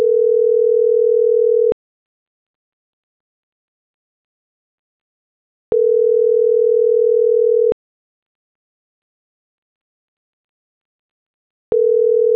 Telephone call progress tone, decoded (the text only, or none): ringback tone